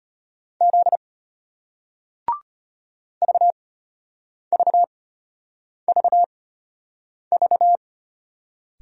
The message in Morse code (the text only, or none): Z VVVV